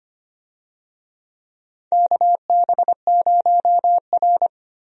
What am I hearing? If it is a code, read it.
Morse code, heard as a keyed tone, about 25 wpm, 700 Hz. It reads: KB0R
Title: USA Call Signs - 4 Characters - Repeated 3x - 25wpm